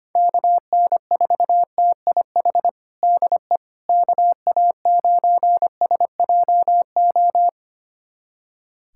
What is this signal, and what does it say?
Morse code, heard as a keyed tone, about 25 wpm, 700 Hz. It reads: KN4TIH DE KA9SJO
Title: USA Callsign DE Callsign - 6 Characters - 25wpm